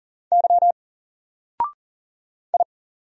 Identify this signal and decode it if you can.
Morse code, heard as a keyed tone, about 40 wpm, 700 Hz. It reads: Y I
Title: Single Letter-Number 40wpm